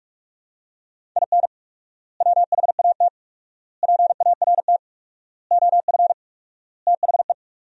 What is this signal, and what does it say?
Morse code, heard as a keyed tone, about 46 wpm, 700 Hz. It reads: IN WHAT PART OF THE
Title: 4x Word-Spacing - Sentences from Top 200 Words - 45wpm